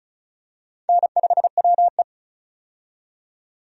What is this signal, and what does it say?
Morse code, heard as a keyed tone, about 36 wpm, 700 Hz. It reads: N5WE
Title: USA Call Signs - 4 Characters - 35wpm